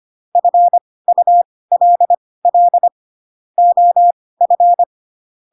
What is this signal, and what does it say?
Morse code, heard as a keyed tone, about 25 wpm, spaced as for 18 wpm, 700 Hz. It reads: FULL OF